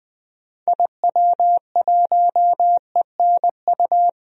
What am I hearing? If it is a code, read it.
Morse code, heard as a keyed tone, about 20 wpm, 700 Hz. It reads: IW1ENU
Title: DXCC-100 Call Signs - 6 Characters - 20wpm